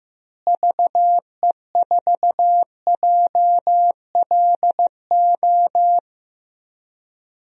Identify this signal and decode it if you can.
Morse code, heard as a keyed tone, about 15 wpm, 700 Hz. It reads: VE4JLO
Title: DXCC-100 Call Signs - 6 Characters - Repeated 2x - 15wpm